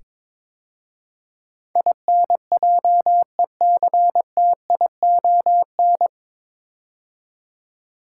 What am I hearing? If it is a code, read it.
Morse code, heard as a keyed tone, about 22 wpm, 700 Hz. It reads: INJECTION
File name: 9 Letter Words - First 3 Letters Given - 22wpm